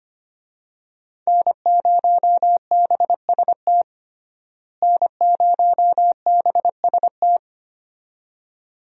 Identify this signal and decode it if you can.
Morse code, heard as a keyed tone, about 25 wpm, 700 Hz. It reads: N0BST N0BST
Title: USA Call Signs - 5 Characters - Repeated 2x - 25wpm